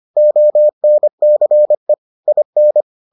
Morse code, heard as a keyed tone, about 25 wpm, 600 Hz. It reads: ONCE IN